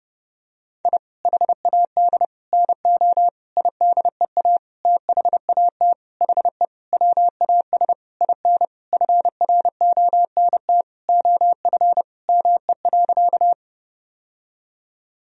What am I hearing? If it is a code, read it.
Morse code, heard as a keyed tone, about 30 wpm, 700 Hz. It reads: I HAD NO IDEA THAT HE WAS IN FRONT OF ME.
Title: Sentences from Top 1000 Words - Review Part 2 - 30wpm